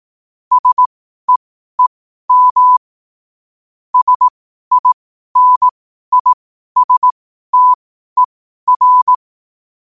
Morse code, heard as a keyed tone, about 18 wpm, spaced as for 12 wpm, 1000 Hz. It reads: SEEM SINISTER
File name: qotdfiles / 12wpm